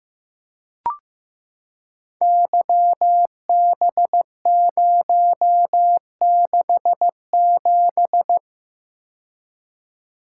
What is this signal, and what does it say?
Morse code, heard as a keyed tone, about 15 wpm, 700 Hz. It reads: YB067
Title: Code Groups - 5 Letter:Numbers - 15wpm